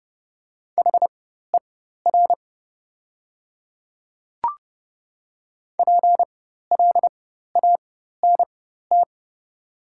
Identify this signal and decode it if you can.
Morse code, heard as a keyed tone, about 30 wpm, spaced as for 14 wpm, 700 Hz. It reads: HER PLANT
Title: 4x Character Spacing - Top 500 Words - 30wpm